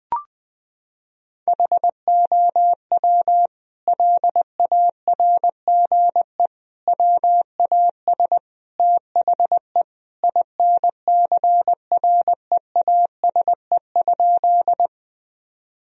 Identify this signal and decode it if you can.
Morse code, heard as a keyed tone, about 20 wpm, 700 Hz. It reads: HOW LARGE WAS THE INCREASE?
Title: Sentences from Top 1000 Words - Review Part 4 - 20wpm